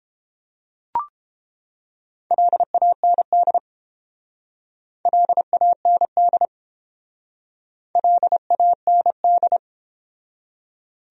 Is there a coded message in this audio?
Yes: Morse code from 2.3 to 9.6 s, heard as a keyed tone, about 30 wpm, 700 Hz. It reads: LAND LAND LAND